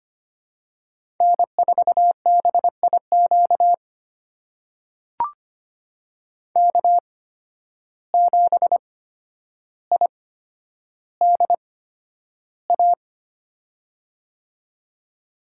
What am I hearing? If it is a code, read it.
Morse code, heard as a keyed tone, about 25 wpm, spaced as for 11 wpm, 700 Hz. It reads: N4BIQ K7IDA
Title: USA Call Signs - 5 Characters - 8x Character Spacing - 25wpm